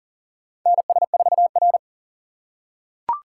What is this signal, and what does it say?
Morse code, heard as a keyed tone, about 37 wpm, 700 Hz. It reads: NS4R
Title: USA Call Signs - 4 Characters - 6x Character Spacing - 40wpm